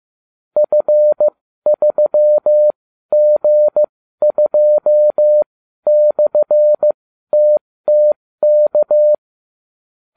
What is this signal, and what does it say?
Morse code, heard as a keyed tone, about 15 wpm, spaced as for 12 wpm, 600 Hz. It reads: F3G2/TTK